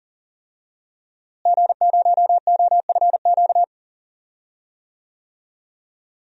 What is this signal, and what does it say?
Morse code, heard as a keyed tone, about 40 wpm, 700 Hz. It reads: G0OFQ
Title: DXCC-100 Call Signs - 5 Characters - Repeated 2x - 40wpm